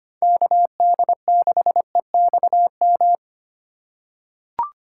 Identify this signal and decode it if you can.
Morse code, heard as a keyed tone, about 25 wpm, 700 Hz. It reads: KD6EXM
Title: USA Call Signs - 6 Characters - 8x Character Spacing - 25wpm